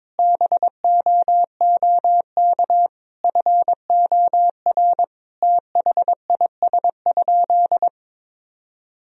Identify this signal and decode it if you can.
Morse code, heard as a keyed tone, about 22 wpm, 700 Hz. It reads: BOOK FOR THIS?
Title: Sentences from Top 1000 Words - Review Part 4 - 22wpm